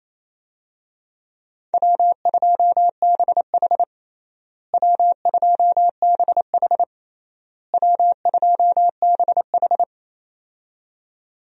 Morse code, heard as a keyed tone, about 28 wpm, 700 Hz. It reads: W2BH W2BH W2BH
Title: USA Call Signs - 4 Characters - Repeated 3x - 28wpm